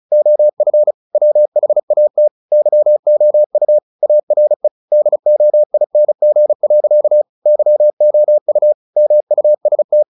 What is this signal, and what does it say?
Morse code, heard as a keyed tone, about 34 wpm, 600 Hz. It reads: OF WHAT YOU ARE DOING. YOU MUST
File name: the old man and the sea / the old man and the sea 1 35wpm